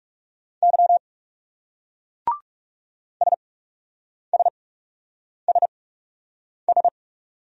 Morse code, heard as a keyed tone, about 36 wpm, spaced as for 8 wpm, 700 Hz. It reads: Y SSSS
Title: Speed-Racing Single Letters 30wpm